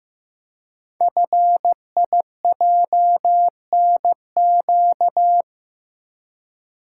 Morse code, heard as a keyed tone, about 15 wpm, 700 Hz. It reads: FIJNQ
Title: Code Groups - 5 Letters - 15wpm